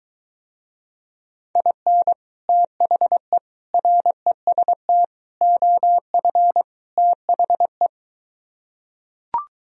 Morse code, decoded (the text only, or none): IN THE REST OF THE